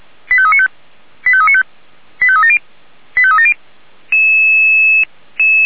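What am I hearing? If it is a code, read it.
Five-tone selective call (ZVEI), heard as quick runs of five tones, tones of about 70 ms. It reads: 85395 85395 85370 85370
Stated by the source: Received by SDR as a ZVEI five-tone selective call.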